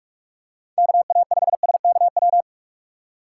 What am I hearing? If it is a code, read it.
Morse code, heard as a keyed tone, about 43 wpm, 700 Hz. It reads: KA5SKW
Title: USA Call Signs - 6 Characters - 4x Character Spacing - 45wpm